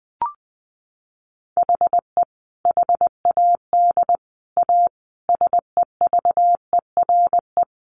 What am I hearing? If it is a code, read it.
Morse code, heard as a keyed tone, about 20 wpm, 700 Hz. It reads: HE HAD A SEVERE
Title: Sentences from Top 1400 Words 20wpm